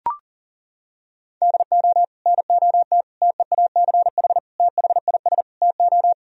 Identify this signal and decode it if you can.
Morse code, heard as a keyed tone, about 37 wpm, 700 Hz. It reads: DO NOT TEACH THIS TO
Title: Sentences from Top 1000 Words - Review Part 2 - 40wpm